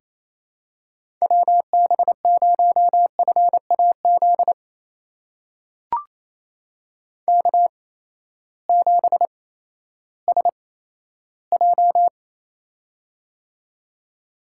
Morse code, heard as a keyed tone, about 28 wpm, spaced as for 14 wpm, 700 Hz. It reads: WB0FAZ K7SJ
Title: USA Call Signs - 8x Character Spacing - 28wpm